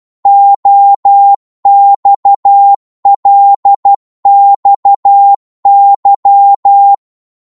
Morse code, heard as a keyed tone, about 12 wpm, 800 Hz. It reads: OXLXY
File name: Morse / ltr-12wpm